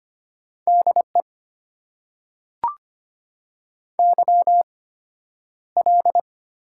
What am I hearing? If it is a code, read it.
Morse code, heard as a keyed tone, about 25 wpm, spaced as for 11 wpm, 690 Hz. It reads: DE YL
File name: Top 90 QSO Elements - 8x Character Spacing - 25wpm